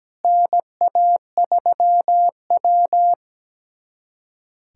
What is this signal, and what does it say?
Morse code, heard as a keyed tone, about 17 wpm, 700 Hz. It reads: NA3W
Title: USA Callsign DE Callsign - 4 Characters - 17wpm